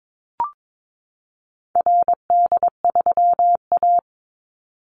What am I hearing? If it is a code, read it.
Morse code, heard as a keyed tone, about 22 wpm, 700 Hz. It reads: RD3A
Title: Call Signs Repeated 2x - 22wpm